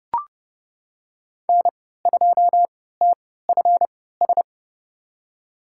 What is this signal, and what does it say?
Morse code, heard as a keyed tone, about 32 wpm, spaced as for 17 wpm, 700 Hz. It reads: N2TFS
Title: USA Call Signs - 5 Characters - 3x Character Spacing - 30wpm